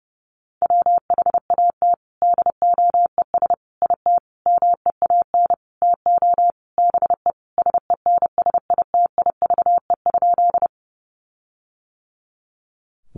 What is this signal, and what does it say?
Morse code, heard as a keyed tone, about 30 wpm, 700 Hz. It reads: WHAT DOES IT MEAN TO BE SENSITIVE?
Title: Sentences from Top 1400 Words 30wpm